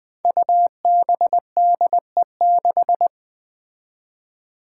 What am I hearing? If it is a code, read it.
Morse code, heard as a keyed tone, about 20 wpm, 700 Hz. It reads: UBDE6
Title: Five Challenging Letter-Number Combinations 20wpm